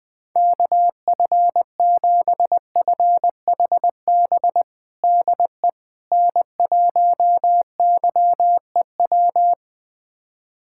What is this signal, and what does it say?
Morse code, heard as a keyed tone, about 20 wpm, 700 Hz. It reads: KF7FHB DE N1YEW
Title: USA Callsign DE Callsign 20wpm